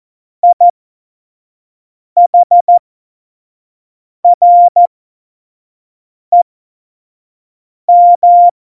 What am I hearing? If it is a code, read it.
Morse code, heard as a keyed tone, about 14 wpm, spaced as for 5 wpm, 700 Hz. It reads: IHREM
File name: Morsekurs / Witze / Witz 03. September 2019, 25 F70